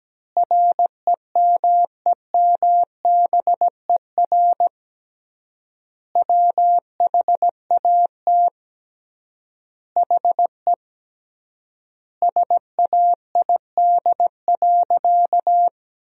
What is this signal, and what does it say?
Morse code, heard as a keyed tone, about 17 wpm, 700 Hz. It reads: REMEMBER WHAT HE SAID.
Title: 3x Word-Spacing - Sentences from Top 400 Words - 17wpm